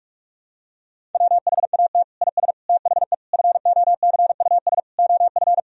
Morse code, heard as a keyed tone, about 41 wpm, 700 Hz. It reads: WHAT IS THE FOCUS OF
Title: Sentences from Top 1000 Words - Review Part 4 - 45wpm